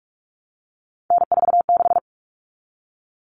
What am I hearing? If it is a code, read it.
Morse code, heard as a keyed tone, about 44 wpm, 700 Hz. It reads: N46
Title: All Cut Numbers - 3-Digit 45wpm